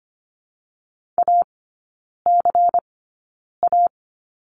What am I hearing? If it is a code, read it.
Morse code, heard as a keyed tone, about 25 wpm, spaced as for 9 wpm, 700 Hz. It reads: ACA
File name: course / 25:10wpm / 093 Lesson 16e - Letter c - Three Characters 25wpm